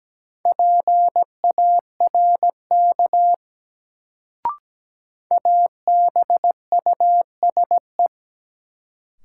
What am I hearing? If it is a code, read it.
Morse code, heard as a keyed tone, about 17 wpm, 700 Hz. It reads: PARK ABUSE
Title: Sentences from Top 1300 Words - New Word Drill Alt - 17wpm